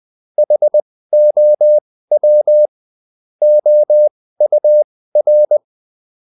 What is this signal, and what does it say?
Morse code, heard as a keyed tone, about 20 wpm, spaced as for 15 wpm, 600 Hz. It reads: HOW OUR
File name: Top-100-Words-15